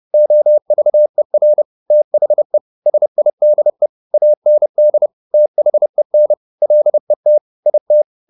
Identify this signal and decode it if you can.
Morse code, heard as a keyed tone, about 30 wpm, 600 Hz. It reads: OVER THE SIDE AND THEN LET IT